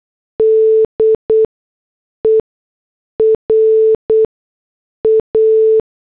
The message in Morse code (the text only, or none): D E R A